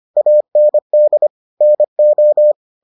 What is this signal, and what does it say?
Morse code, heard as a keyed tone, about 25 wpm, 600 Hz. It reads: AND NO